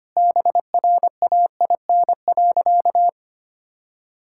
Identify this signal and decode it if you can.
Morse code, heard as a keyed tone, about 25 wpm, 700 Hz. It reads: BRAIN.
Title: Sentences from Top 1000 Words - Review Part 6 - 25wpm